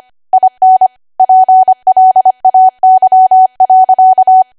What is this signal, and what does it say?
Morse code, heard as a keyed tone, about 25 wpm, 750 Hz. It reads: IN PLAY.
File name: cwa / W1AW-Level3 / 25WPM